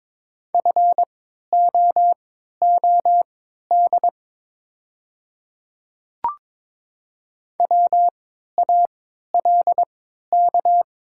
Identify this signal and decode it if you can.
Morse code, heard as a keyed tone, about 23 wpm, spaced as for 13 wpm, 700 Hz. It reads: FOOD WALK
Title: 3x Character Spacing - Top 300 Words - 22wpm